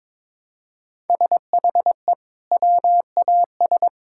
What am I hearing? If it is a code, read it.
Morse code, heard as a keyed tone, about 22 wpm, 700 Hz. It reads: SHE WAS
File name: practice / 22wpm / Sentences with 2-Word Phrases - 22wpm